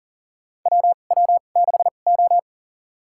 Morse code, heard as a keyed tone, about 40 wpm, spaced as for 29 wpm, 700 Hz. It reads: WW6O
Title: USA Call Signs - 4 Characters - 2x Character Spacing - 40wpm